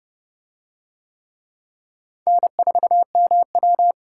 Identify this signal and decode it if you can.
Morse code, heard as a keyed tone, about 29 wpm, 700 Hz. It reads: N4MW